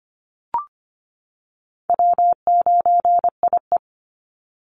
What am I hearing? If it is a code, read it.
Morse code, heard as a keyed tone, about 25 wpm, 700 Hz. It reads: W9IE
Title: Easy Call Signs - 25wpm